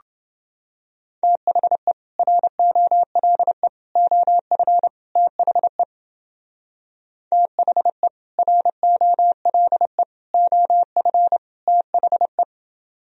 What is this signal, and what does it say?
Morse code, heard as a keyed tone, about 30 wpm, 700 Hz. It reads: THE ROLE OF THE THE ROLE OF THE